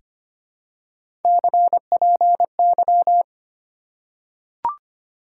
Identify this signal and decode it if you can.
Morse code, heard as a keyed tone, about 25 wpm, 700 Hz. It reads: CPY
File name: Top 90 QSO Elements - 2x Character Spacing - 25wpm